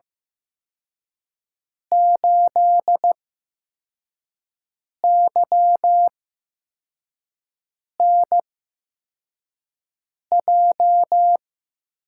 Morse code, heard as a keyed tone, about 16 wpm, spaced as for 4 wpm, 700 Hz. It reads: 8YNJ
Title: USA Call Signs - 6 Characters - 8x Character Spacing - 15wpm